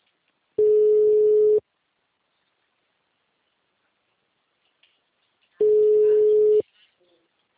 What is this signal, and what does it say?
Ringback tone (Europe, 425 Hz), with speech.